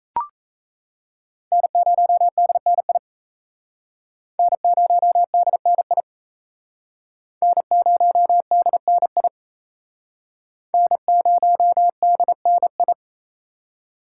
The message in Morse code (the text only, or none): N0DNI N0DNI N0DNI N0DNI